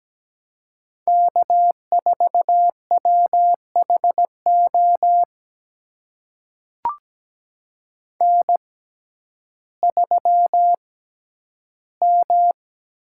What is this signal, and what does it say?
Morse code, heard as a keyed tone, about 17 wpm, spaced as for 10 wpm, 700 Hz. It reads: K4WHO N3M
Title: USA Call Signs - 5 Characters - 6x Character Spacing - 17wpm